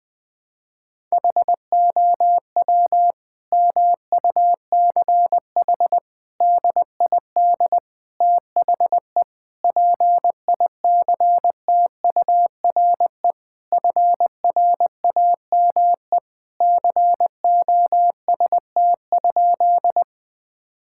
Morse code, heard as a keyed tone, about 20 wpm, 700 Hz. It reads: HOW MUCH DID THE PICTURE FRAME COST?
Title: Sentences from Top 1000 Words - Review Part 6 - 20wpm